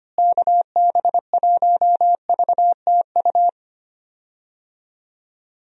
Morse code, heard as a keyed tone, about 25 wpm, 700 Hz. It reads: KB1VTU
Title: USA Call Signs - 6 Characters - Repeated 3x - 25wpm